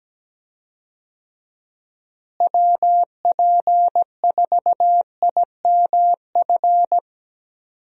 Morse code, heard as a keyed tone, about 17 wpm, 700 Hz. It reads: WP4IMF
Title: USA Call Signs - 6 Characters - Repeated 2x - 17wpm